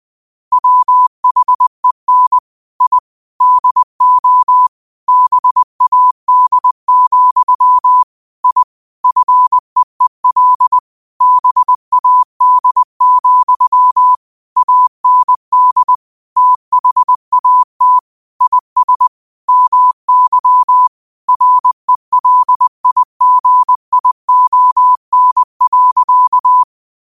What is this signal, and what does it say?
Morse code, heard as a keyed tone, about 20 wpm, 1000 Hz. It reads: WHEN I DO BAD, I FEEL BAD, AND THAT IS MY RELIGION.
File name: qotdfiles / 20wpm